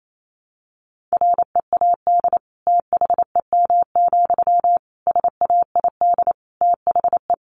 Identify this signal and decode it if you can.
Morse code, heard as a keyed tone, about 28 wpm, 700 Hz. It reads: READ THEM, SAID THE
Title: Sentences from Top 400 Words 28wpm